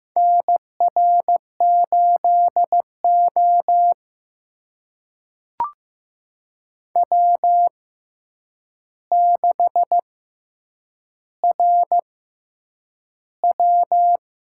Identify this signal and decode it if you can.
Morse code, heard as a keyed tone, about 15 wpm, spaced as for 8 wpm, 700 Hz. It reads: NR8O W6RW
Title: USA Call Signs - 6x Character Spacing - 15wpm